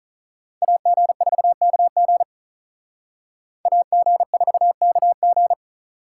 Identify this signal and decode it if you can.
Morse code, heard as a keyed tone, about 44 wpm, 700 Hz. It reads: AG4KG AG4KG